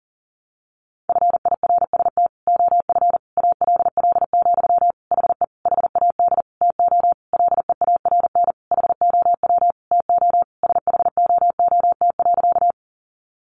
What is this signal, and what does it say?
Morse code, heard as a keyed tone, about 40 wpm, 700 Hz. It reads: FIRST OF ALL, HE HAD TO LEARN HOW TO SHOOT.
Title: Sentences from Top 1200 Words 40wpm